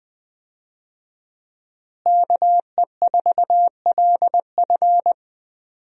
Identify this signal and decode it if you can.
Morse code, heard as a keyed tone, about 20 wpm, 700 Hz. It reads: KE4LF
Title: USA Call Signs - 5 Characters - Repeated 2x - 20wpm